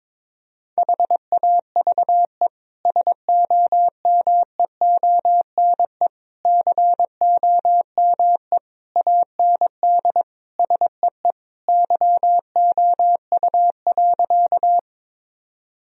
Morse code, heard as a keyed tone, about 22 wpm, 700 Hz. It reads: HAVE SOMEONE COME AND SEE YOU.
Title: Sentences from Top 1000 Words - Review Part 4 - 22wpm